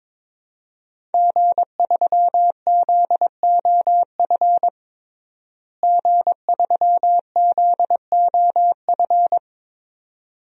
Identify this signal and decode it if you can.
Morse code, heard as a keyed tone, about 22 wpm, 700 Hz. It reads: G3ZOF G3ZOF
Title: DXCC-100 Call Signs - 5 Characters - Repeated 2x - 22wpm